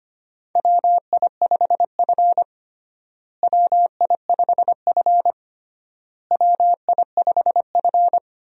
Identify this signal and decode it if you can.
Morse code, heard as a keyed tone, about 25 wpm, 700 Hz. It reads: WI5F WI5F WI5F
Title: USA Call Signs - 4 Characters - Repeated 3x - 25wpm